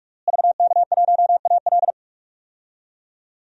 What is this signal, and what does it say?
Morse code, heard as a keyed tone, about 42 wpm, 700 Hz. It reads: VK1AL
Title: DXCC-100 Call Signs - 5 Characters - Repeated 3x - 45wpm